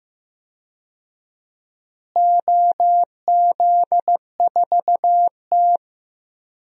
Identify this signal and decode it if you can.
Morse code, heard as a keyed tone, about 15 wpm, 700 Hz. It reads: OZ4T
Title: DXCC-100 Call Signs - 4 Characters - Repeated 2x - 15wpm